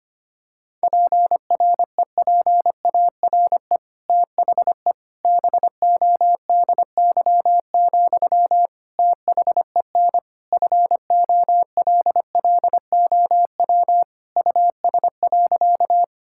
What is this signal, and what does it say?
Morse code, heard as a keyed tone, about 25 wpm, 700 Hz. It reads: PREPARE THE BODY, THEN FOLLOW US.
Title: Sentences from Top 1000 Words - Review Part 6 - 25wpm